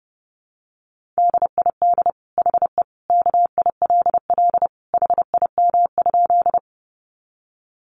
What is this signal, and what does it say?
Morse code, heard as a keyed tone, about 30 wpm, 700 Hz. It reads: DID HE KILL HIM?
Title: Sentences from Top 1400 Words 30wpm